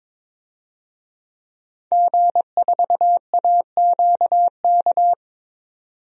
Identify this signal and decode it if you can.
Morse code, heard as a keyed tone, about 22 wpm, 700 Hz. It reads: G4AQK
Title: DXCC-100 Call Signs - 5 Characters - 22wpm